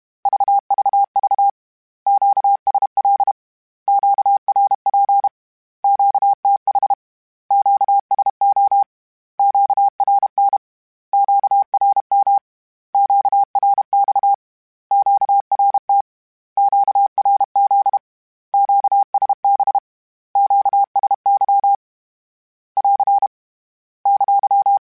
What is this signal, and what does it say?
Morse code, heard as a keyed tone, about 32 wpm, 800 Hz. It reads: VVV QSL QRP QTH QSO QRN QRM QRX QRT QRZ QSB QSY <AR> !
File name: Q Calls / Q Calls-mp3s / Q Calls Once @30